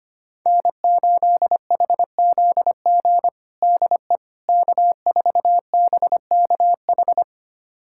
Morse code, heard as a keyed tone, about 25 wpm, 700 Hz. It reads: N8HZG DE K4BKH
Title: USA Callsign DE Callsign 25wpm